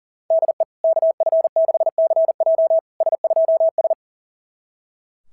Morse code, heard as a keyed tone, about 37 wpm, 690 Hz. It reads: DE KF6CJ S2S
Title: SOTA QSO Practice 40wpm